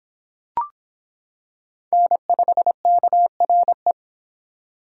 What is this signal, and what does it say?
Morse code, heard as a keyed tone, about 25 wpm, 700 Hz. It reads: N5KRE